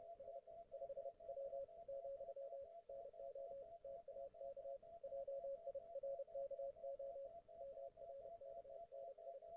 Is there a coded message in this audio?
Yes: Morse code from 0.2 to 9.4 s, heard as a keyed tone, about 30 wpm, 550 Hz. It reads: S 5W, NO TAK JIRKO MRKNI